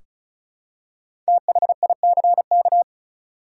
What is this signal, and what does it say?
Morse code, heard as a keyed tone, about 36 wpm, 700 Hz. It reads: THICK
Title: Five Letter Words - 35wpm